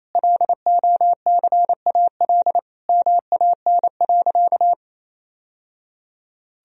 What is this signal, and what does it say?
Morse code, heard as a keyed tone, about 27 wpm, 700 Hz. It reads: LOCAL MAN.
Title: Sentences from Top 1000 Words - Review Part 3 - 28wpm